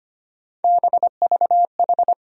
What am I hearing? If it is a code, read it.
Morse code, heard as a keyed tone, about 25 wpm, 700 Hz. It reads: BVH
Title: USA Call Sign Postfixes - 3 Letters 25wpm